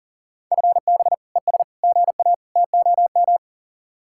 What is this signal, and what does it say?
Morse code, heard as a keyed tone, about 49 wpm, 700 Hz. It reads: FB ES GA TOM
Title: Speed-Racing Common QSO Phrases - 40wpm